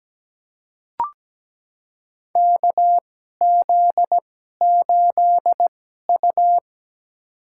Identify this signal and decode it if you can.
Morse code, heard as a keyed tone, about 17 wpm, spaced as for 12 wpm, 700 Hz. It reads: KZ8U
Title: USA Call Signs - 4 Characters - 2x Character Spacing - 17wpm